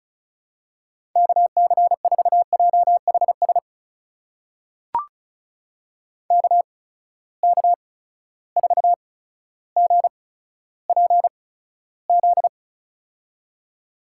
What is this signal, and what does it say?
Morse code, heard as a keyed tone, about 36 wpm, spaced as for 15 wpm, 700 Hz. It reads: KC4JHS KK4GPZ
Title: USA Call Signs - 6 Characters - 8x Character Spacing - 35wpm